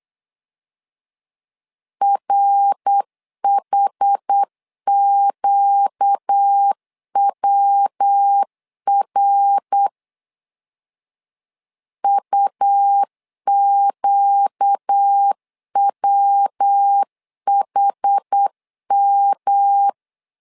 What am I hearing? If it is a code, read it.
Morse code, heard as a keyed tone, about 8 wpm, 800 Hz. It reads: RHQWR UQWHM
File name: 4 Q 8wpm